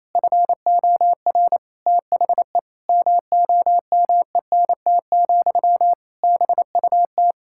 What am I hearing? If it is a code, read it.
Morse code, heard as a keyed tone, about 28 wpm, 700 Hz. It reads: FOR THE MOMENT, BUT
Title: Sentences from Top 1000 Words - Review Part 6 - 28wpm